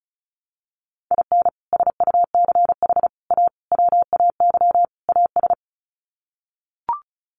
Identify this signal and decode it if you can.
Morse code, heard as a keyed tone, about 34 wpm, 700 Hz. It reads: IN SUCH A WAY AS
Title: Sets of 5 Words Encore 35wpm